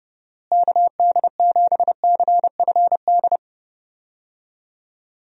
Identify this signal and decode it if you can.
Morse code, heard as a keyed tone, about 30 wpm, 700 Hz. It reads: KD7CFD
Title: USA Call Signs - 6 Characters - Repeated 3x - 30wpm